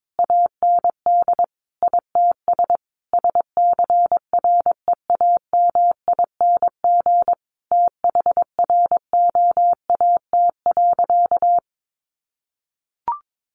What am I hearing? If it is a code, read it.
Morse code, heard as a keyed tone, about 22 wpm, 700 Hz. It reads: AND ITS SCREAMING THROAT.
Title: A Sound Of Thunder 22wpm